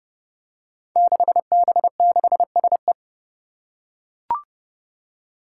Morse code, heard as a keyed tone, about 30 wpm, 700 Hz. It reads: 6B6SE
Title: Five Challenging Letter-Number Combinations 30wpm